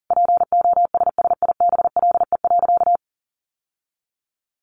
Morse code, heard as a keyed tone, about 40 wpm, 700 Hz. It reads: POSSIBLE.
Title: Sentences from Top 900 Words 40wpm